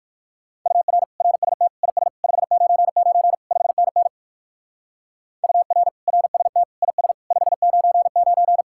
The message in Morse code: UR RST IS 599 5NN UR RST IS 599

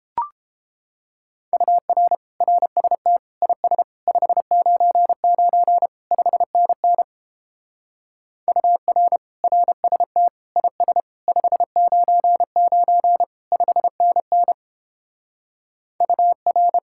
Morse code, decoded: UR RST IS 599 5NN UR RST IS 599 5NN UR

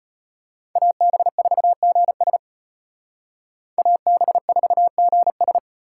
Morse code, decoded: AB4GS AB4GS

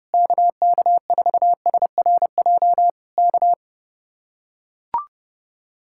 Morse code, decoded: KK4SRJ K

